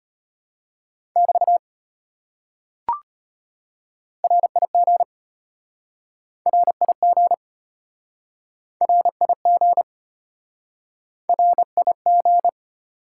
Morse code, heard as a keyed tone, about 33 wpm, 700 Hz. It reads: <BT> RIG RIG RIG RIG